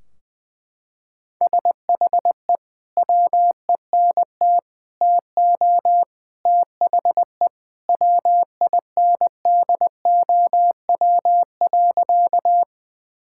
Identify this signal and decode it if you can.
Morse code, heard as a keyed tone, about 19 wpm, 700 Hz. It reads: SHE WENT TO THE WINDOW.
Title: Sentences from Top 1600 Words 20wpm